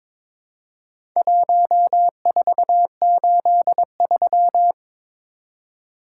Morse code, heard as a keyed tone, about 21 wpm, 700 Hz. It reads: 1483
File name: practice / 22wpm / Four Letter-Number Combinations - 22wpm